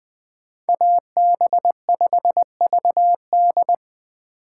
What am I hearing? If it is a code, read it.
Morse code, heard as a keyed tone, about 21 wpm, 700 Hz. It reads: AB5VD